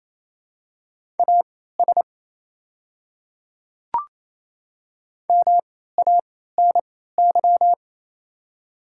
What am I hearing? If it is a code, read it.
Morse code, heard as a keyed tone, about 29 wpm, spaced as for 16 wpm, 700 Hz. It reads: AS MANY
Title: 3x Character Spacing - Top 100 Words - 28wpm